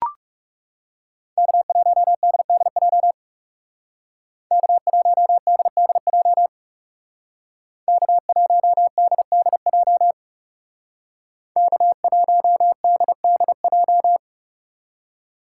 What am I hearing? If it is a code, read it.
Morse code, heard as a keyed tone, about 37 wpm, 700 Hz. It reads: K1DDJ K1DDJ K1DDJ K1DDJ
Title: Speed Racing - USA 1x3 Call Signs 30wpm